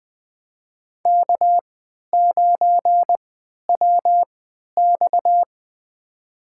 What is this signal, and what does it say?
Morse code, heard as a keyed tone, about 20 wpm, spaced as for 11 wpm, 700 Hz. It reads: K9WX
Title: USA Call Signs - 4 Characters - 3x Character Spacing - 20wpm